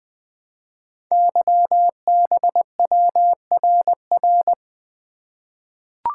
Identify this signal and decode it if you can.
Morse code, heard as a keyed tone, about 20 wpm, 700 Hz. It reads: YBWRR
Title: Code Groups - 5 Letters - 20wpm